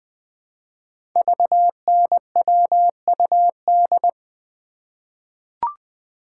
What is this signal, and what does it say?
Morse code, heard as a keyed tone, about 20 wpm, 700 Hz. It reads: VNWUD
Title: Code Groups - 5 Letters - 20wpm